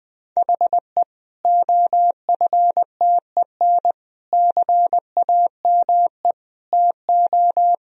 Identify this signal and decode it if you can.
Morse code, heard as a keyed tone, about 20 wpm, 700 Hz. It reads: HE OFTEN CAME TO
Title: Sentences from Top 1000 Words - Review Part 1 - 20wpm